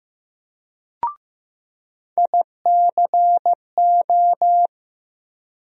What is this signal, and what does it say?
Morse code, heard as a keyed tone, about 15 wpm, 700 Hz. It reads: ICO
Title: USA Call Sign Postfixes - 3 Letters 15wpm